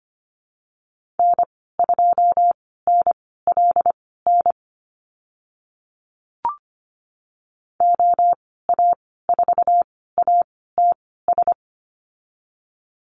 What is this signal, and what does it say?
Morse code, heard as a keyed tone, about 25 wpm, spaced as for 16 wpm, 700 Hz. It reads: N2NLN OA4ATS
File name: Lesson 15g - Number 5 - Call Signs 25wpm